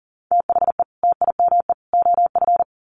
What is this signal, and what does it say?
Morse code, heard as a keyed tone, about 40 wpm, 700 Hz. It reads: THE TIME OF
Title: Sets of 3 Words 40wpm